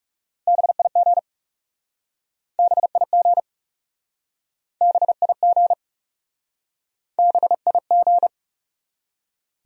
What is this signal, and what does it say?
Morse code, heard as a keyed tone, about 40 wpm, 700 Hz. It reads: BIG BIG BIG BIG